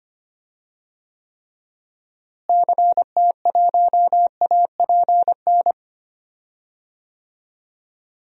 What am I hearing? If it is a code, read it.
Morse code, heard as a keyed tone, about 25 wpm, 700 Hz. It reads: CT1APN
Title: DXCC-100 Call Signs - 6 Characters - Repeated 2x - 25wpm